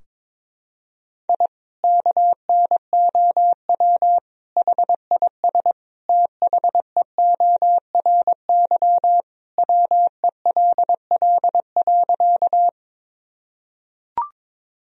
Morse code, heard as a keyed tone, about 22 wpm, 700 Hz. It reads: I KNOW HIS THEORY WELL.
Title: Sentences from Top 1000 Words - Review Part 3 - 22wpm